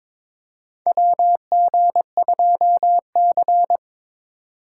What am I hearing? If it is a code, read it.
Morse code, heard as a keyed tone, about 22 wpm, 700 Hz. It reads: WG2C
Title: USA Call Signs - 4 Characters - 3x Character Spacing - 22wpm